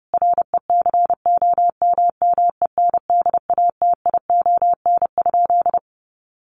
Morse code, heard as a keyed tone, about 30 wpm, 700 Hz. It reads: RECOMMENDATION?
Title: Sentences from Top 1400 Words 30wpm